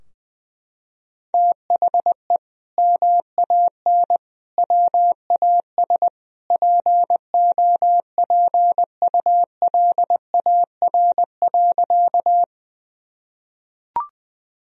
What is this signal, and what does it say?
Morse code, heard as a keyed tone, about 20 wpm, 700 Hz. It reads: THE MAN WAS POPULAR.
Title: Sentences from Top 1000 Words - Review Part 3 - 20wpm